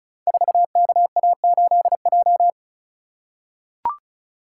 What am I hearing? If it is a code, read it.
Morse code, heard as a keyed tone, about 36 wpm, 690 Hz. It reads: 4KA8J
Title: Code Groups - 5 Letter:Numbers - 35wpm